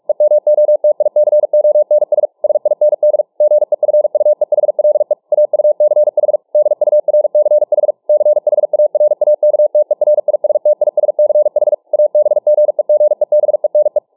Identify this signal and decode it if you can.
Morse code, heard as a keyed tone, about 45 wpm, 600 Hz. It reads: EMOTICONS SIND GEFUEHLE AUCH DURCH CHARAKTERISTISCH ABGEGEBENE